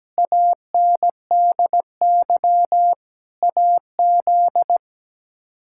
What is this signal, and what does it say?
Morse code, heard as a keyed tone, about 17 wpm, 700 Hz. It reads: ANDY AZ
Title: K1USN SST QSO Practice - 17wpm